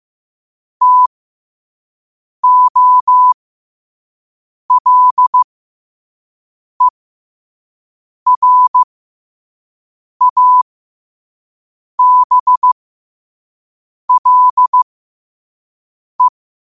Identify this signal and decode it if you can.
Morse code, heard as a keyed tone, about 15 wpm, spaced as for 5 wpm, 1000 Hz. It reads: TOLERABLE
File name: qotdfiles / 5wpm